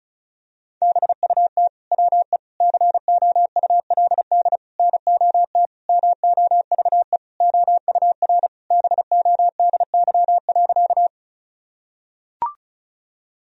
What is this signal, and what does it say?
Morse code, heard as a keyed tone, about 34 wpm, 700 Hz. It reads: BUT WE COULD NOT MOVE OUR BODY.
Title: Sentences from Top 1000 Words - Review Part 2 - 35wpm